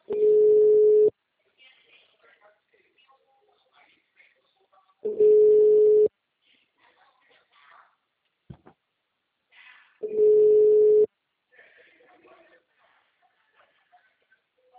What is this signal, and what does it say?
Ringback tone (Europe, 425 Hz), with speech.
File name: out- -2006- - - .60906